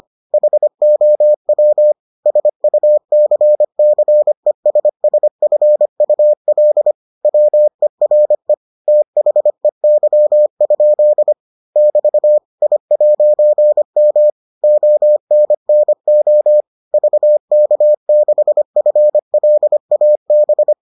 Morse code, heard as a keyed tone, about 25 wpm, 600 Hz. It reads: HOW SUCCESSFUL WERE THEY? <BT> I'M ONNO VK6FLAB